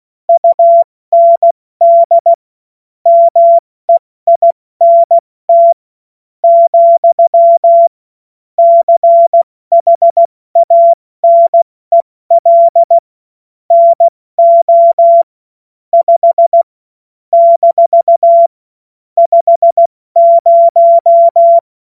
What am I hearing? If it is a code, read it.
Morse code, heard as a keyed tone, about 16 wpm, spaced as for 13 wpm, 680 Hz. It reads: UND MEINT, CHANEL NO 5 - 50
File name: Morsekurs / Witze / Witz 31. August 2021, 70 F80